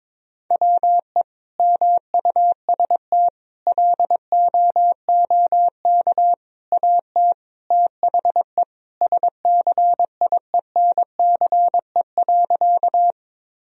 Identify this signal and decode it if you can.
Morse code, heard as a keyed tone, about 22 wpm, 700 Hz. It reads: WE MUST LOOK AT THE SCIENCE.
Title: Sentences from Top 1000 Words - Review Part 2 - 22wpm